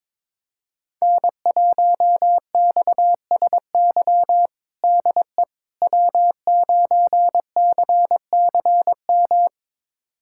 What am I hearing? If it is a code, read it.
Morse code, heard as a keyed tone, about 22 wpm, 700 Hz. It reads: N1XSY DE W9CCM
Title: USA Callsign DE Callsign - 5 Characters - 22wpm